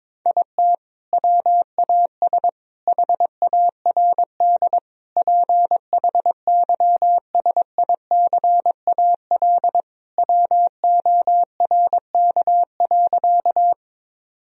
Morse code, heard as a keyed tone, about 21 wpm, 700 Hz. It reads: IT WAS HARD PHYSICAL WORK.
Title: Sentences from Top 1000 Words - Review Part 4 - 22wpm